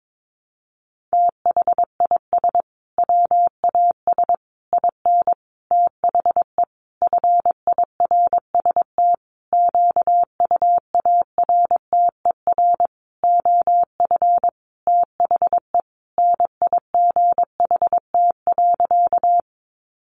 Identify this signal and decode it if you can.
Morse code, heard as a keyed tone, about 22 wpm, 700 Hz. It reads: THIS WAS IN THE FIRST QUARTER OF THE NIGHT.
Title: Sentences from Top 1300 Words 22wpm